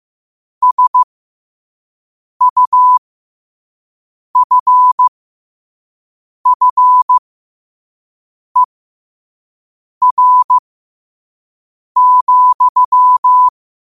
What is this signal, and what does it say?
Morse code, heard as a keyed tone, about 15 wpm, spaced as for 5 wpm, 1000 Hz. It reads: SUFFER,
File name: qotdfiles / 5wpm